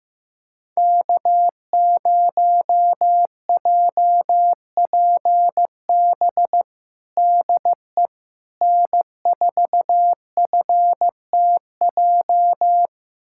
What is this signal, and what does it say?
Morse code, heard as a keyed tone, about 16 wpm, 700 Hz. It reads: K0JPB DE N4FTJ
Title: USA Callsign DE Callsign - 5 Characters - 15wpm